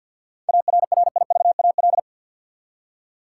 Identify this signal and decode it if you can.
Morse code, heard as a keyed tone, about 46 wpm, 700 Hz. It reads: ARRIVAL
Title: Seven Letter Words - 50wpm